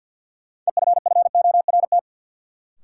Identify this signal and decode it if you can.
Morse code, heard as a keyed tone, about 52 wpm, 700 Hz. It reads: EFFORT
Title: Sentences from Top 1200 Words - New Word Drill - 50wpm